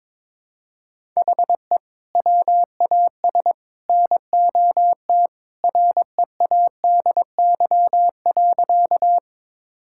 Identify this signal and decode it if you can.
Morse code, heard as a keyed tone, about 22 wpm, 700 Hz. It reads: HE WAS NOT READY.
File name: practice / 22wpm / Sentences with 3-Word Phrases - 22wpm